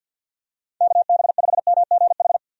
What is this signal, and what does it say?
Morse code, heard as a keyed tone, about 51 wpm, 700 Hz. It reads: KB5MGH
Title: USA Call Signs - 6 Characters - 50wpm